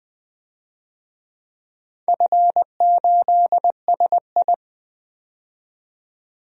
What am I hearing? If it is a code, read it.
Morse code, heard as a keyed tone, about 20 wpm, 700 Hz. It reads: F8SI
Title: DXCC-100 Call Signs - 4 Characters - 20wpm